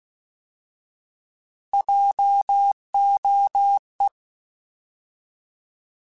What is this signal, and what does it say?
Morse code, heard as a keyed tone, about 16 wpm, 800 Hz. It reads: JOE
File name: Alphabetical / Alphabetical mp3s / Js @15